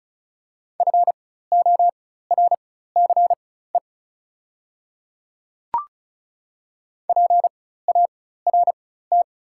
Morse code, heard as a keyed tone, about 36 wpm, spaced as for 16 wpm, 690 Hz. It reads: FORCE PART
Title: 4x Character Spacing - Top 500 Words - 35wpm